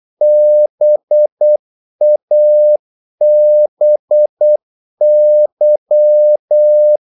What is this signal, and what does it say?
Morse code, heard as a keyed tone, about 8 wpm, 600 Hz. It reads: BABY
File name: Korte ord DK 8wpm